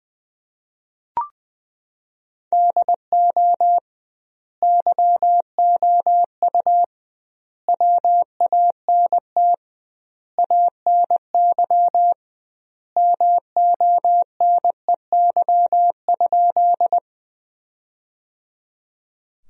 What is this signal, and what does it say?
Morse code, heard as a keyed tone, about 21 wpm, 700 Hz. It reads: DO YOU WANT ANY MONEY?
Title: 2x Word-Spacing - Sentences from Top 400 Words - 20wpm